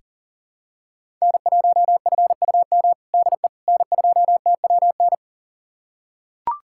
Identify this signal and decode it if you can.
Morse code, heard as a keyed tone, about 40 wpm, 700 Hz. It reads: N1FUM DE N2TWN
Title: USA Callsign DE Callsign - 5 Characters - 40wpm